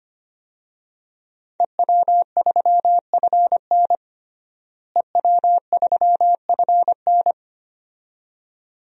Morse code, heard as a keyed tone, about 25 wpm, 700 Hz. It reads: EW3FN EW3FN
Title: DXCC-100 Call Signs - 5 Characters - Repeated 2x - 25wpm